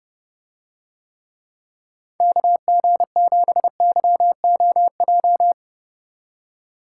Morse code, heard as a keyed tone, about 29 wpm, 700 Hz. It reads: KG7YOJ